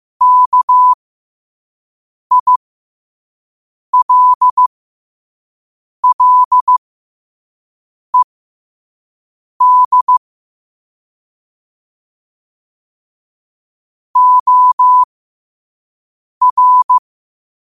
Morse code, heard as a keyed tone, about 15 wpm, spaced as for 5 wpm, 1000 Hz. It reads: KILLED OR
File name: qotdfiles / 5wpm